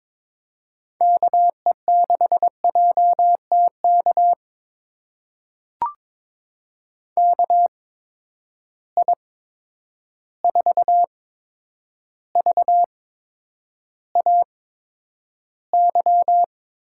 Morse code, heard as a keyed tone, about 22 wpm, spaced as for 9 wpm, 700 Hz. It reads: KE6JTK KI4VAY
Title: USA Call Signs - 6 Characters - 8x Character Spacing - 22wpm